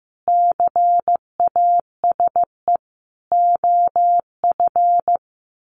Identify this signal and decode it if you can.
Morse code, heard as a keyed tone, about 15 wpm, 700 Hz. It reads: CASE OF
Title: Sets of 5 Words 15wpm